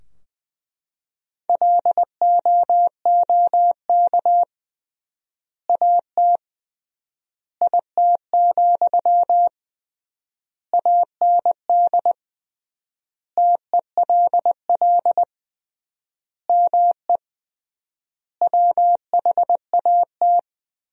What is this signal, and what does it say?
Morse code, heard as a keyed tone, about 21 wpm, 700 Hz. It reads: LOOK AT IT, AND TELL ME WHAT
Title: 3x Word-Spacing - Sentences from Top 200 Words - 20wpm